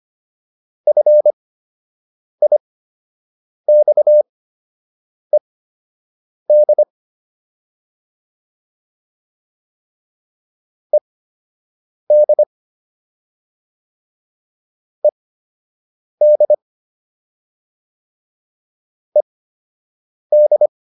Morse code, heard as a keyed tone, about 25 wpm, spaced as for 7 wpm, 600 Hz. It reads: FIXED ED ED ED